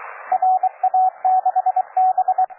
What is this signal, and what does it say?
Morse code, heard as a keyed tone, about 22 wpm, 750 Hz, with radio noise. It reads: RA6B
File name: cw forum / OJ0B sur 1823 le 21-09-2008 17h00 test GP XXL